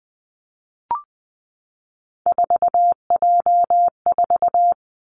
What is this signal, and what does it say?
Morse code, heard as a keyed tone, about 20 wpm, 700 Hz. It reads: 4J4
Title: Three Challenging Letter-Number Combinations 20wpm